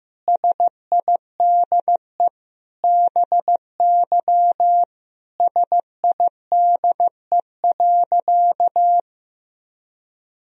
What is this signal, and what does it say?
Morse code, heard as a keyed tone, about 15 wpm, 700 Hz. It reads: SIDE BY SIDE.
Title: Sentences from Top 1000 Words - Review Part 2 - 15wpm